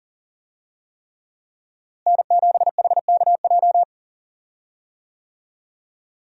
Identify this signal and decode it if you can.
Morse code, heard as a keyed tone, about 40 wpm, 700 Hz. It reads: N7HKJ
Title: USA Call Signs - 5 Characters - Repeated 3x - 40wpm